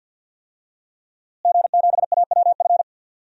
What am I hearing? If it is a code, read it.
Morse code, heard as a keyed tone, about 45 wpm, 700 Hz. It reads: G7AWF
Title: DXCC-100 Call Signs - 5 Characters - Repeated 2x - 50wpm